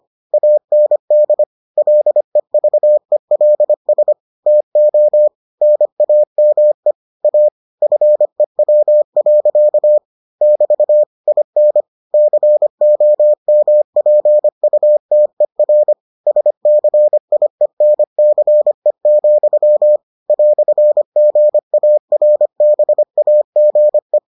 Morse code, heard as a keyed tone, about 25 wpm, 600 Hz. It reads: AND LEVELS TO NAME A FEW. <BT> IN COMPUTER SCIENCE, "GARBAGE